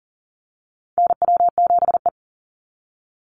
Morse code, heard as a keyed tone, about 40 wpm, 700 Hz. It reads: NW7E
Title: NA SOTA Chaser Callsigns 40wpm